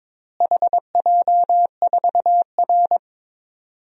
Morse code, heard as a keyed tone, about 22 wpm, 700 Hz. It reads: HJ4R
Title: DXCC-100 Call Signs - 4 Characters - Repeated 2x - 22wpm